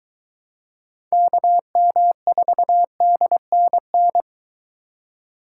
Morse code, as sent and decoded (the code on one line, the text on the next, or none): -.- -- ....- -.. -. -.
KM4DNN